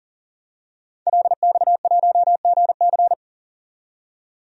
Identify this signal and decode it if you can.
Morse code, heard as a keyed tone, about 40 wpm, 700 Hz. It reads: LX1GC
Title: DXCC-100 Call Signs - 5 Characters - 40wpm